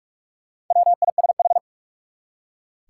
Morse code, heard as a keyed tone, about 46 wpm, 700 Hz. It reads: WISH
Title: Four Letter Words - 45wpm